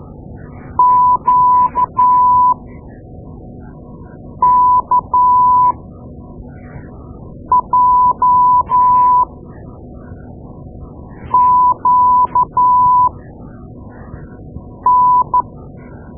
Morse code, heard as a keyed tone, about 12 wpm, spaced as for 4 wpm, 1000 Hz. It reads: QKJQN
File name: Morse Lessons / Morse Lessons / Tape-2A